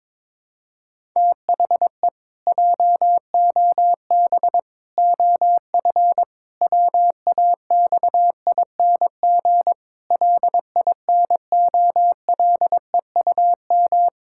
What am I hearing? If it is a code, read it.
Morse code, heard as a keyed tone, about 22 wpm, 700 Hz. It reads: THE JOB OF WAXING LINOLEUM
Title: Pangram Challenge v2 - 22wpm